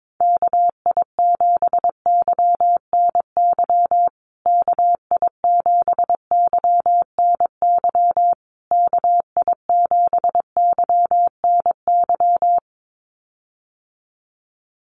Morse code, heard as a keyed tone, about 22 wpm, 700 Hz. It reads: KI7YNY KI7YNY KI7YNY
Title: USA Call Signs Repeated 3x 22wpm